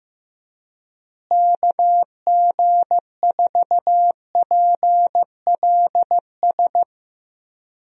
Morse code, heard as a keyed tone, about 15 wpm, 700 Hz. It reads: KG4PLS
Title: USA Call Signs - 6 Characters - Repeated 3x - 15wpm